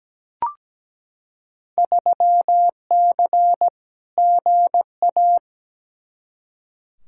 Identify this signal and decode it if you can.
Morse code, heard as a keyed tone, about 17 wpm, 700 Hz. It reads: 3C GA
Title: Field Day Exchanges Encore - 17wpm